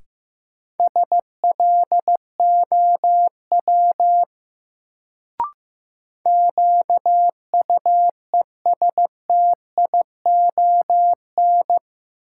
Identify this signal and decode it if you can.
Morse code, heard as a keyed tone, about 15 wpm, 700 Hz. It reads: SLOW QUESTION
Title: Sentences from Top 400 Words - New Word Drill - 15wpm